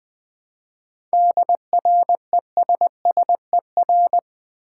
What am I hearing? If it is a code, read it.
Morse code, heard as a keyed tone, about 20 wpm, 700 Hz. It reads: DRESSER
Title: Easy Words with Hints - 20wpm